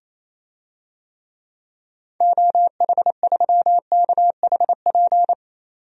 Morse code, heard as a keyed tone, about 28 wpm, 700 Hz. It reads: OH3KHP
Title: DXCC-100 Call Signs - 6 Characters - 28wpm